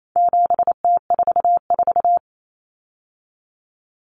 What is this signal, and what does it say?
Morse code, heard as a keyed tone, about 28 wpm, 700 Hz. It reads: 7T44
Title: Common Cut Numbers - 4-Digit 28wpm